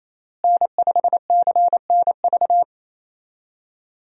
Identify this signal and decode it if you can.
Morse code, heard as a keyed tone, about 28 wpm, 700 Hz. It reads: N5CNV
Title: USA Call Signs - 5 Characters - Repeated 3x - 28wpm